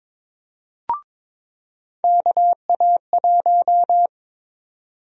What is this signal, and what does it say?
Morse code, heard as a keyed tone, about 21 wpm, 700 Hz. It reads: KA1
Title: USA Call Sign Prefixes - 22wpm